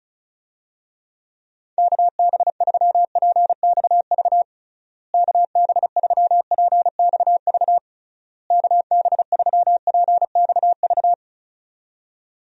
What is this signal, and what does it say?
Morse code, heard as a keyed tone, about 36 wpm, 700 Hz. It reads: KB3PXV KB3PXV KB3PXV
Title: USA Call Signs - 6 Characters - Repeated 3x - 35wpm